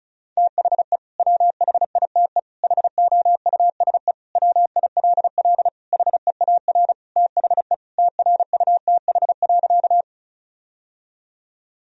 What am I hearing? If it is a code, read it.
Morse code, heard as a keyed tone, about 36 wpm, 700 Hz. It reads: THE WHITE HOUSE WILL HEAR THE TRUTH.
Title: Sentences from Top 1000 Words - Review Part 1 - 35wpm